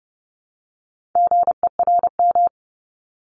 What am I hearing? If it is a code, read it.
Morse code, heard as a keyed tone, about 30 wpm, 700 Hz. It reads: GERM
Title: Multi-letter Phonemes Repeated 3x 30wpm